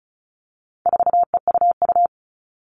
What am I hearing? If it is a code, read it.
Morse code, heard as a keyed tone, about 34 wpm, 700 Hz. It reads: 4EUU
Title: All Cut Numbers - 4-Digit 35wpm